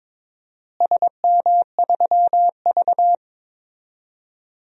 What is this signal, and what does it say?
Morse code, heard as a keyed tone, about 22 wpm, 700 Hz. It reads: SM3V
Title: DXCC-100 Call Signs - 4 Characters - Repeated 3x - 22wpm